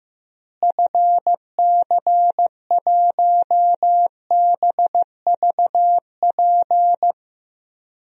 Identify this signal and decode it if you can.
Morse code, heard as a keyed tone, about 16 wpm, 700 Hz. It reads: FC1BVP
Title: DXCC-100 Call Signs - 6 Characters - Repeated 3x - 15wpm